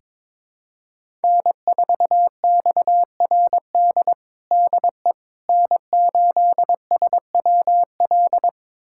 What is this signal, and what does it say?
Morse code, heard as a keyed tone, about 22 wpm, 700 Hz. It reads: N4XRD DE N8SWL
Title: USA Callsign DE Callsign 22wpm